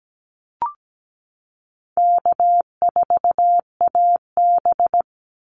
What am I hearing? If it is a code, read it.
Morse code, heard as a keyed tone, about 17 wpm, 700 Hz. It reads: K4AB
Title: Easy Call Signs - 17wpm